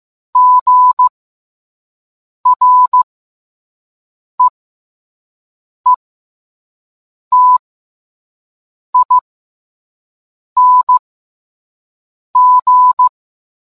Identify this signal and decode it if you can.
Morse code, heard as a keyed tone, about 15 wpm, spaced as for 5 wpm, 1000 Hz. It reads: GREETING